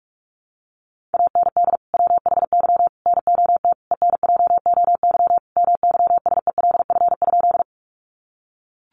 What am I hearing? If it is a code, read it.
Morse code, heard as a keyed tone, about 44 wpm, 700 Hz. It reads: AND WHY NOT ENJOY MYSELF?
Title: Sentences from Top 1300 Words 45wpm